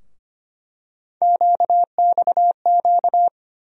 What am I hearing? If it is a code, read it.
Morse code, heard as a keyed tone, about 25 wpm, 700 Hz. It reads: QXQ
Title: USA Call Sign Postfixes - 3 Letters 25wpm